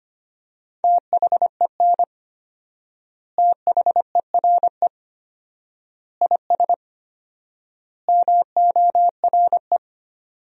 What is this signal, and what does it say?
Morse code, heard as a keyed tone, about 25 wpm, 700 Hz. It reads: THEN THERE IS MORE
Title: 4x Word-Spacing - Sentences from Top 100 Words - 25wpm